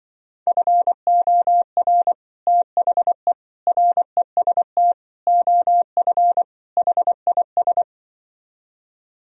Morse code, heard as a keyed tone, about 24 wpm, 700 Hz. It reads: FOR THE REST OF HIS
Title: Speed-Racing - Sets of 5 Words 20wpm